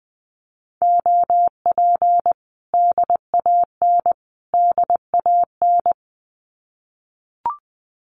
Morse code, heard as a keyed tone, about 20 wpm, 700 Hz. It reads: OP DAN DAN
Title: Common QSO Phrases Encore 20wpm